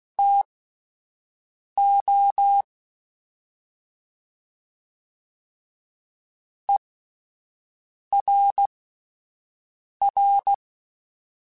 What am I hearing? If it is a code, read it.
Morse code, heard as a keyed tone, about 16 wpm, spaced as for 5 wpm, 800 Hz. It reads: TO ERR
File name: Spaced / Chuckles Spaced-mp3s / HumanErr sp @10